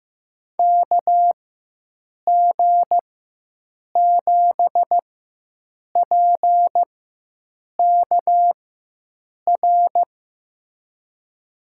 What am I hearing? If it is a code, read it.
Morse code, heard as a keyed tone, about 15 wpm, spaced as for 7 wpm, 700 Hz. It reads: KG7PKR
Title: USA Call Signs - 6 Characters - 4x Character Spacing - 15wpm